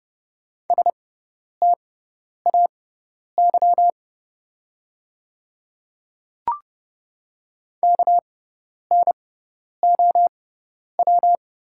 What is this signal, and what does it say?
Morse code, heard as a keyed tone, about 30 wpm, spaced as for 10 wpm, 700 Hz. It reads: STAY KNOW
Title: 6x Character Spacing - Top 500 Words - 30wpm